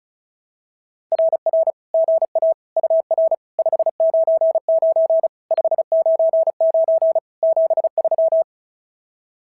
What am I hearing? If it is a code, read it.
Morse code, heard as a keyed tone, about 31 wpm, 660 Hz. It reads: RR GA UR 599 599 73
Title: SOTA QSO Practice 35wpm